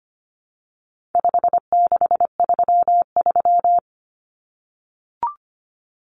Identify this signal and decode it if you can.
Morse code, heard as a keyed tone, about 25 wpm, 700 Hz. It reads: 5633
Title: Numbers - 4-Digit 25wpm